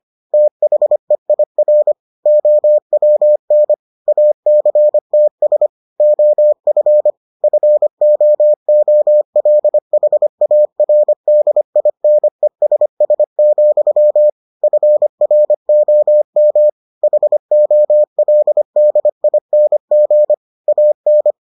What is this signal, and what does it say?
Morse code, heard as a keyed tone, about 25 wpm, 600 Hz. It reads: THEIR OWN ACTS OF FOOLHARDINESS, FROM HOLDING AN